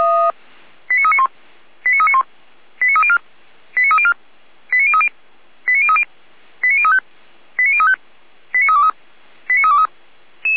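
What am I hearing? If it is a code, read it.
Five-tone selective call (ZVEI), heard as quick runs of five tones, tones of about 70 ms. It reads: 89391 89391 89304 89304 89030 89030 89035 89035 89323 89323
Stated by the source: ZVEI five-tone selective call received on SDR.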